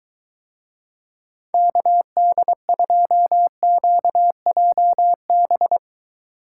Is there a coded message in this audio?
Yes: Morse code from 1.5 to 5.8 s, heard as a keyed tone, about 23 wpm, 700 Hz. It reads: KD2QJB